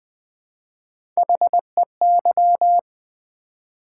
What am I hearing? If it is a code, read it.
Morse code, heard as a keyed tone, about 20 wpm, 700 Hz. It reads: HEY
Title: Build A Word - 1 Letter At A Time - 3-Letter Words - 20wpm